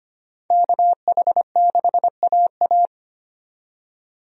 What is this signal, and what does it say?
Morse code, heard as a keyed tone, about 25 wpm, 700 Hz. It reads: KH6AA
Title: DXCC-100 Call Signs - 5 Characters - 25wpm